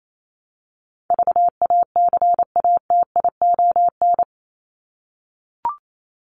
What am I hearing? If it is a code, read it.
Morse code, heard as a keyed tone, about 28 wpm, 700 Hz. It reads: VACATION
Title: Top 1501-1750 Words 28wpm